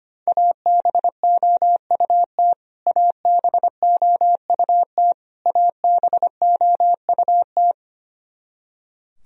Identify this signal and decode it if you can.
Morse code, heard as a keyed tone, about 25 wpm, 700 Hz. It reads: ABOUT ABOUT ABOUT
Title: Top 100 Words Repeated 25wpm